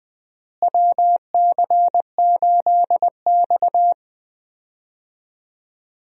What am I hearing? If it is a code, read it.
Morse code, heard as a keyed tone, about 20 wpm, 700 Hz. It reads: WC8X
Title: USA Call Signs - 4 Characters - Repeated 3x - 20wpm